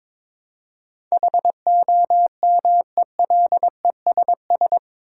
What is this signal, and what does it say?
Morse code, heard as a keyed tone, about 22 wpm, 700 Hz. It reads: HOMELESS
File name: practice / 22wpm / Eight Letter Words - 22wpm